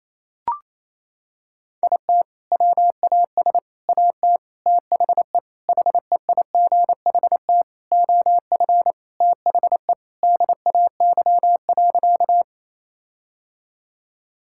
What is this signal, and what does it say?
Morse code, heard as a keyed tone, about 28 wpm, 700 Hz. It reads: IT WAS AT THE HEIGHT OF THE DAY.
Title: Sentences from Top 1600 Words 28wpm